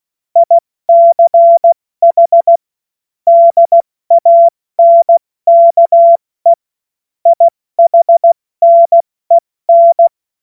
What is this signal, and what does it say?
Morse code, heard as a keyed tone, about 16 wpm, spaced as for 14 wpm, 680 Hz. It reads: ICH DANKE IHNEN
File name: Morsekurs / Witze / Witz 06. Dezember 2022 70 F80